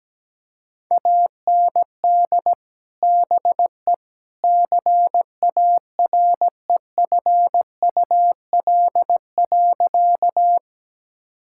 Morse code, heard as a keyed tone, about 17 wpm, 700 Hz. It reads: AND BE CAREFUL.
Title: Sentences from Top 1600 Words 17wpm